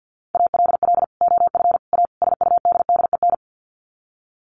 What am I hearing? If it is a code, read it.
Morse code, heard as a keyed tone, about 48 wpm, 700 Hz. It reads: ALL OF A SUDDEN
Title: Sets of 4 Words 50wpm